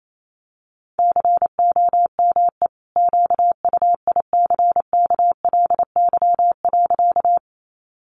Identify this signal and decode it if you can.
Morse code, heard as a keyed tone, about 28 wpm, 700 Hz. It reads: COME QUICKLY.
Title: Sentences from Top 700 Words 28wpm